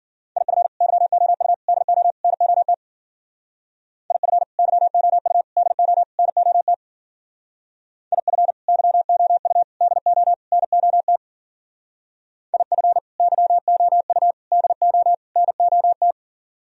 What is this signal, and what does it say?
Morse code, heard as a keyed tone, about 50 wpm, 700 Hz. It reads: IF YOU DO NOT IF YOU DO NOT IF YOU DO NOT IF YOU DO NOT
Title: Speed-Racing - Sets of 4 Words 40wpm